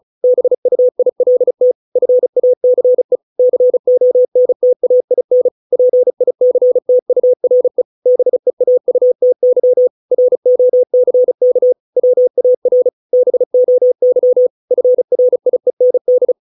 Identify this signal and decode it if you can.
Morse code, heard as a keyed tone, about 34 wpm, 500 Hz. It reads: BUILT FACE CONTAIN PICTURE BEAUTY ROCK WAR BOY FRIEND